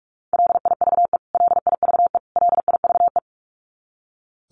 Morse code, heard as a keyed tone, about 44 wpm, 700 Hz. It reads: LIVE LIVE LIVE
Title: Top 200 Words Repeated 45wpm